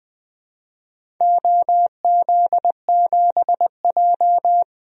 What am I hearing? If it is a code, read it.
Morse code, heard as a keyed tone, about 20 wpm, 700 Hz. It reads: OZ7J
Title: DXCC-100 Call Signs - 4 Characters - Repeated 2x - 20wpm